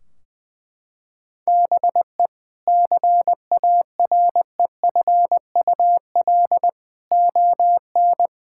Morse code, heard as a keyed tone, about 21 wpm, 700 Hz. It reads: BE CAREFUL ON
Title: Sentences from Top 1600 Words 20wpm